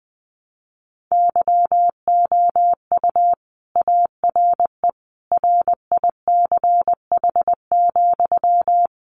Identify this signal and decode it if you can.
Morse code, heard as a keyed tone, about 20 wpm, 700 Hz. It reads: YOU ARE RICH,
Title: Sentences from Top 1100 Words 20wpm